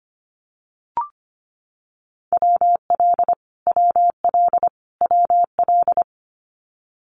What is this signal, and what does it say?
Morse code, heard as a keyed tone, about 25 wpm, 700 Hz. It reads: WL WL WL
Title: Abbreviations Repeated 3x - 25wpm